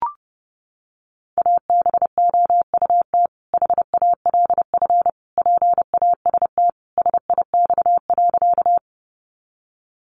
Morse code, heard as a keyed tone, about 30 wpm, 700 Hz. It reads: ABOUT HALF PAST SIX.
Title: Sentences from Top 600 Words 30wpm